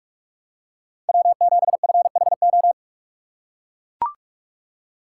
Morse code, heard as a keyed tone, about 43 wpm, 700 Hz. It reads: W7FHO
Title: USA Call Signs - 6x Character Spacing - 45wpm